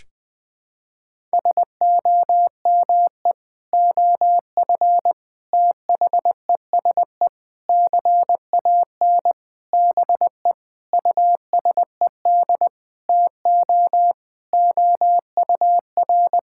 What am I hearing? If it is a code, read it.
Morse code, heard as a keyed tone, about 21 wpm, 700 Hz. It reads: SOME OF THESE CAN BE USED TO OUR
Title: Sentences from Top 1000 Words - Review Part 6 - 20wpm